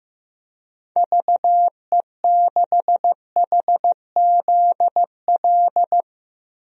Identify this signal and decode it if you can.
Morse code, heard as a keyed tone, about 15 wpm, 700 Hz. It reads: VE6HZL
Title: DXCC-100 Call Signs - 6 Characters - 15wpm